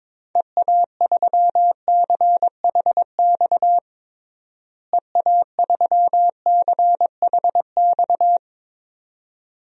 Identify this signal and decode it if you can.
Morse code, heard as a keyed tone, about 22 wpm, 700 Hz. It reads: EA3CHX EA3CHX
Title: DXCC-100 Call Signs - 6 Characters - Repeated 2x - 22wpm